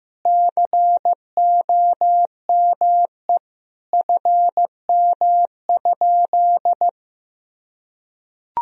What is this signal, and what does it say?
Morse code, heard as a keyed tone, about 15 wpm, 700 Hz. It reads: COME FM?
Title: Sentences from Top 200 Words and Abbreviations 15wpm